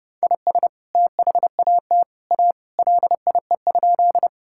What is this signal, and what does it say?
Morse code, heard as a keyed tone, about 32 wpm, 700 Hz. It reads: IS THAT A LIE?
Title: Sentences from Top 1600 Words 30wpm